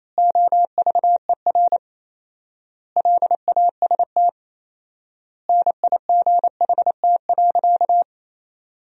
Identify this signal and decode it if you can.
Morse code, heard as a keyed tone, about 27 wpm, 700 Hz. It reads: OVER LAST NIGHT.
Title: 4x Word-Spacing - Sentences from Top 300 Words - 28wpm